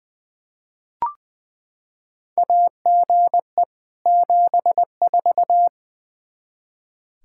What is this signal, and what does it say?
Morse code, heard as a keyed tone, about 20 wpm, 700 Hz. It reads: AGE 74
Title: K1USN SST QSO Practice 20wpm